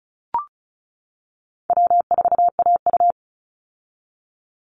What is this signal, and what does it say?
Morse code, heard as a keyed tone, about 34 wpm, 700 Hz. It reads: W4AU
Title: Call Signs - 35wpm